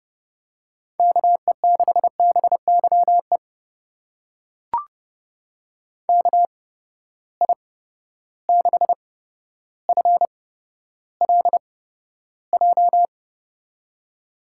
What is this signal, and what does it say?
Morse code, heard as a keyed tone, about 30 wpm, spaced as for 13 wpm, 700 Hz. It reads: KE6BYE KI6FLJ
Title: USA Call Signs - 6 Characters - 8x Character Spacing - 30wpm